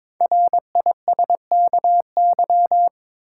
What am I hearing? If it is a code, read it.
Morse code, heard as a keyed tone, about 22 wpm, 700 Hz. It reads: RISKY
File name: 5 Letter Words - First 3 Letters Given - 22wpm